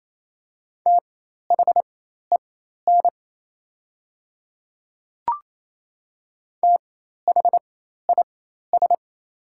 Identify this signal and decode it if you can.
Morse code, heard as a keyed tone, about 28 wpm, spaced as for 13 wpm, 700 Hz. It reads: THEN THIS
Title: 4x Character Spacing - Top 200 Words - 28wpm